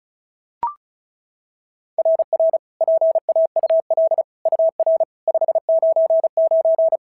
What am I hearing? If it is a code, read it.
Morse code, heard as a keyed tone, about 31 wpm, 660 Hz. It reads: RR PAUL UR 599
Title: SOTA QSO Practice 35wpm